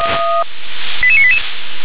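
Five-tone selective call (ZVEI), heard as quick runs of five tones, tones of about 70 ms. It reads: 88988
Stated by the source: ZVEI five-tone selective call 88951 received on SDR.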